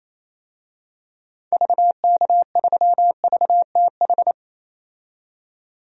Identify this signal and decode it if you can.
Morse code, heard as a keyed tone, about 28 wpm, 700 Hz. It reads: VK3VTH
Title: DXCC-100 Call Signs - 6 Characters - 28wpm